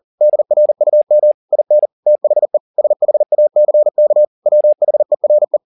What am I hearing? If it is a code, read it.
Morse code, heard as a keyed tone, about 40 wpm, 600 Hz. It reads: DRUM IN THE SHACK WHERE